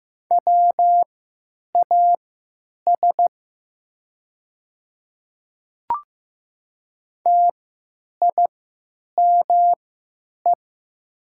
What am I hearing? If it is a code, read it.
Morse code, heard as a keyed tone, about 15 wpm, spaced as for 9 wpm, 700 Hz. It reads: WAS TIME
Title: 3x Character Spacing - Top 100 Words - 15wpm